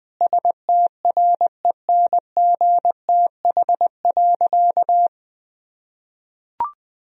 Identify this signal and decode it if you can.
Morse code, heard as a keyed tone, about 20 wpm, 700 Hz. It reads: STRENGTH.
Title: Sentences from Top 1000 Words - Review Part 4 - 20wpm